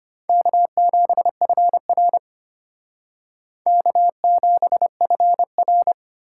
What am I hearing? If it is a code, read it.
Morse code, heard as a keyed tone, about 32 wpm, 700 Hz. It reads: K7FR K7FR